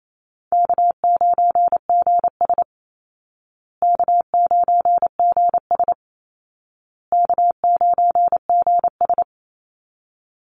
Morse code, heard as a keyed tone, about 28 wpm, 700 Hz. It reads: K9GS K9GS K9GS
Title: Call Signs Repeated 2x - 28wpm